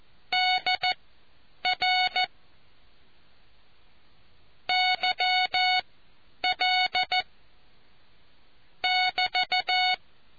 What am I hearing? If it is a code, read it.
Morse code, heard as a keyed tone, about 14 wpm, spaced as for 9 wpm, 740 Hz. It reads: DR YL <BT>